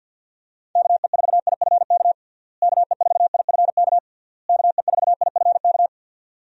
Morse code, heard as a keyed tone, about 51 wpm, 700 Hz. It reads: KE4IFK KE4IFK KE4IFK
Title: USA Call Signs - 6 Characters - Repeated 3x - 50wpm